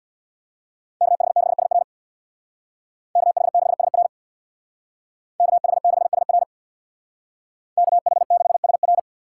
Morse code, heard as a keyed tone, about 58 wpm, 700 Hz. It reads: KH6SR KH6SR KH6SR KH6SR
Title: Speed Racing - USA 2x2 Call Signs 50wpm